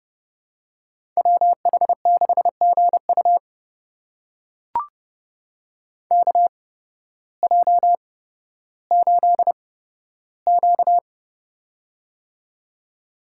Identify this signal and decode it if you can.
Morse code, heard as a keyed tone, about 30 wpm, spaced as for 14 wpm, 700 Hz. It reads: WH6GU KJ8Q
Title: USA Call Signs - 8x Character Spacing - 30wpm